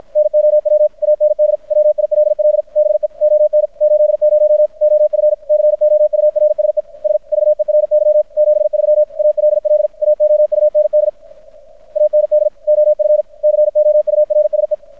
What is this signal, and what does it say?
Morse code, heard as a keyed tone, about 53 wpm, 600 Hz. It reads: NOW AND WILL BE ON 80 OF COURSE S VERY 73 ALL AOUND AND OF COURSE
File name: archived / CRSnet